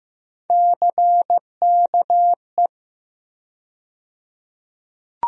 Morse code, heard as a keyed tone, about 15 wpm, 700 Hz. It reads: CKE
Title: USA Call Sign Postfixes - 3 Letters 15wpm